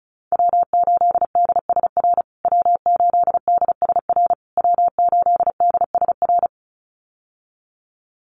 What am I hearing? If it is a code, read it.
Morse code, heard as a keyed tone, about 34 wpm, 700 Hz. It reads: W8DSR W8DSR W8DSR
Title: USA Call Signs Repeated 3x 35wpm